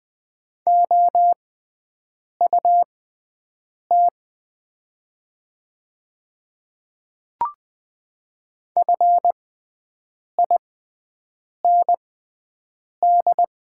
Morse code, heard as a keyed tone, about 20 wpm, spaced as for 7 wpm, 700 Hz. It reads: OUT FIND
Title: 6x Character Spacing - Top 100 Words - 20wpm